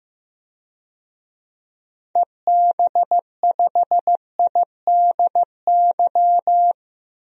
Morse code, heard as a keyed tone, about 15 wpm, 700 Hz. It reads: EB5IDY
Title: DXCC-100 Call Signs - 6 Characters - 15wpm